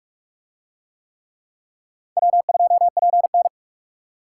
Morse code, heard as a keyed tone, about 44 wpm, 700 Hz. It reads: W2PN